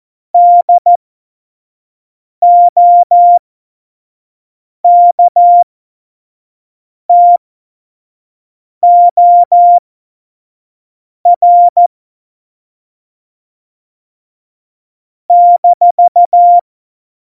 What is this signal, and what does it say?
Morse code, heard as a keyed tone, about 14 wpm, spaced as for 5 wpm, 700 Hz. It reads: DOKTOR -
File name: Morsekurs / Witze / Witz 05. Dezember 2017, 25 F70